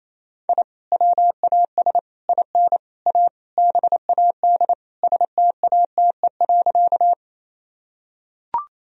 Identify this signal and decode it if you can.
Morse code, heard as a keyed tone, about 28 wpm, 700 Hz. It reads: I WAS IN A BAD STATE.
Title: Sentences from Top 1000 Words - Review Part 2 - 28wpm